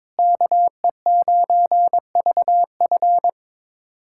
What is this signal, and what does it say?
Morse code, heard as a keyed tone, about 22 wpm, 700 Hz. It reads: KE9VF
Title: USA Callsign DE Callsign - 5 Characters - 22wpm